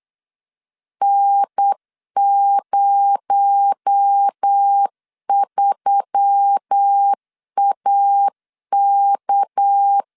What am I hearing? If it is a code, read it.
Morse code, heard as a keyed tone, about 8 wpm, 800 Hz. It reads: N03AK
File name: CW / DET / 1 6 1500C S B 8wpm 5142